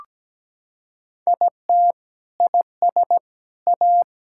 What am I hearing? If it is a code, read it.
Morse code, heard as a keyed tone, about 17 wpm, 700 Hz. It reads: IT IS A
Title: Sentences from Top 1000 Words - Review Part 5 - 17wpm